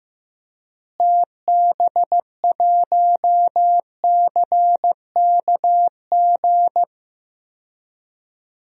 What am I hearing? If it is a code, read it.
Morse code, heard as a keyed tone, about 15 wpm, 700 Hz. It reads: TB1CKG
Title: DXCC-100 Call Signs - 6 Characters - Repeated 3x - 15wpm